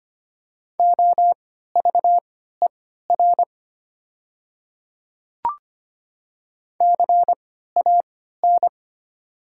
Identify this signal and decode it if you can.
Morse code, heard as a keyed tone, about 25 wpm, spaced as for 14 wpm, 700 Hz. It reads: OVER CAN
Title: 3x Character Spacing - Top 100 Words - 25wpm